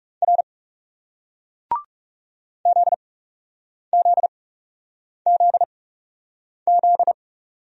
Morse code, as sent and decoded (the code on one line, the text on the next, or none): .-. / --.. --.. --.. --..
R ZZZZ